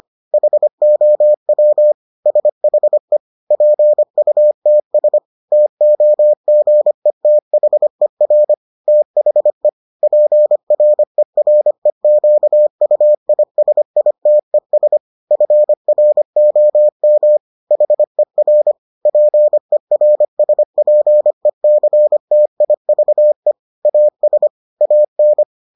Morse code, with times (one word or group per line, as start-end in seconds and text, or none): 0.3-1.9 HOW
2.3-3.2 SHE
3.5-5.2 PUTS
5.5-8.5 TOGETHER
8.9-9.7 THE
10.0-15.0 PREREQUISITES
15.3-17.4 FROM
17.7-18.7 HER
19.1-23.5 PERSPECTIVE
23.9-24.5 AS
24.8-25.4 AN